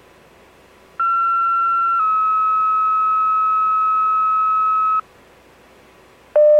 Two-tone page, heard as a one-second tone, then a three-second tone, about 1360 then 1290 Hz. It is heard from 1.0 to 5.0 s.